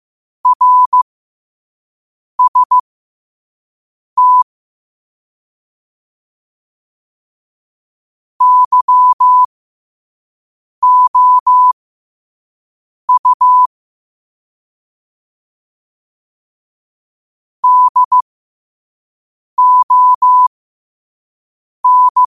Morse code, heard as a keyed tone, about 15 wpm, spaced as for 5 wpm, 1000 Hz. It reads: RST YOU DON
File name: qotdfiles / 5wpm